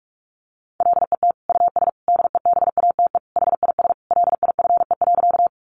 Morse code, heard as a keyed tone, about 46 wpm, 700 Hz. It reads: LET US DEBATE HIS LIFE.
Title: Sentences from Top 1100 Words 45wpm